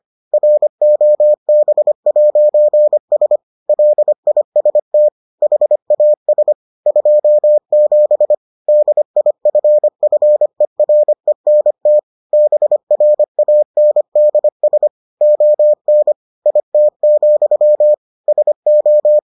Morse code, heard as a keyed tone, about 25 wpm, 600 Hz. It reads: ROB'S LIST HAS 27 DIFFERENT BRANDS ON IT, SO